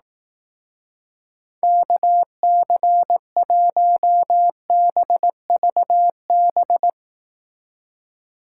Morse code, heard as a keyed tone, about 18 wpm, 700 Hz. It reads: KC1BVB
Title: Speed Racing - USA Call Signs 15wpm